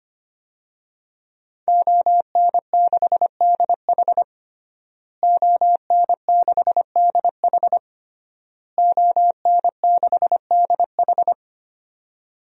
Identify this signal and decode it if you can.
Morse code, heard as a keyed tone, about 25 wpm, 700 Hz. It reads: ON6DH ON6DH ON6DH
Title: DXCC-100 Call Signs - 5 Characters - Repeated 3x - 25wpm